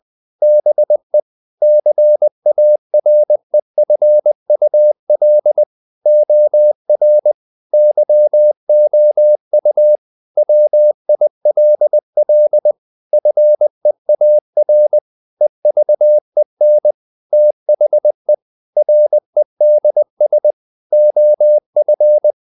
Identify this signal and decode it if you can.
Morse code, heard as a keyed tone, about 20 wpm, 600 Hz. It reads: BE CAREFUL OR YOU WILL FEAR EVEN THE REDS OF
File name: the old man and the sea / the old man and the sea part 1 20wpm